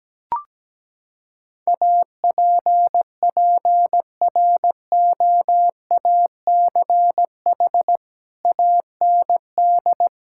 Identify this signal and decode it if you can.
Morse code, heard as a keyed tone, about 17 wpm, 700 Hz. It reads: APPROACH AND
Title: Sentences from Top 1600 Words 17wpm